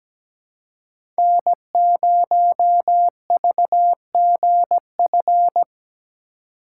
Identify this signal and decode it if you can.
Morse code, heard as a keyed tone, about 17 wpm, 700 Hz. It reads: N0VGF
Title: USA Call Signs - 5 Characters - 3x Character Spacing - 17wpm